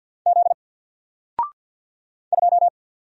Morse code, heard as a keyed tone, about 50 wpm, 700 Hz. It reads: Z 2
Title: Single Letter-Number Alt 50wpm